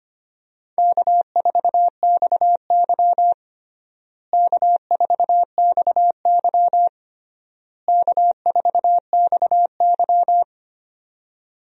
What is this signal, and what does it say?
Morse code, heard as a keyed tone, about 25 wpm, 700 Hz. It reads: K4XY K4XY K4XY
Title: USA Call Signs - 4 Characters - Repeated 3x - 25wpm